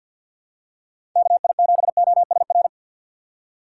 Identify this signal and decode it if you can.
Morse code, heard as a keyed tone, about 53 wpm, 700 Hz. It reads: KI7OSR
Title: USA Call Signs - 6 Characters - 3x Character Spacing - 50wpm